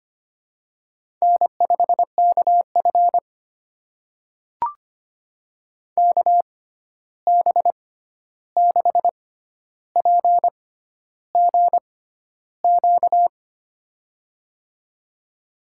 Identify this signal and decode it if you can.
Morse code, heard as a keyed tone, about 25 wpm, spaced as for 11 wpm, 700 Hz. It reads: N5KF KB6PGQ
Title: USA Call Signs - 6x Character Spacing - 25wpm